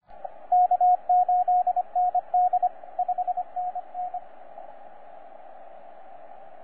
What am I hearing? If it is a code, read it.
Morse code, heard as a keyed tone, about 26 wpm, 720 Hz, with radio noise. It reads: K8ND 5NN
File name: J6/N7QT 20CW